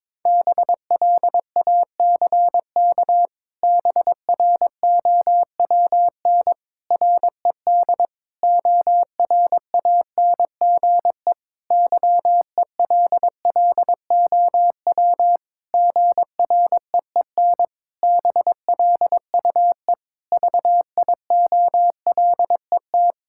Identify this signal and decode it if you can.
Morse code, heard as a keyed tone, about 22 wpm, 700 Hz. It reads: BLACK BROWN RED ORANGE YELLOW GREEN BLUE VIOLET
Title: Pangram Challenge v2 - 22wpm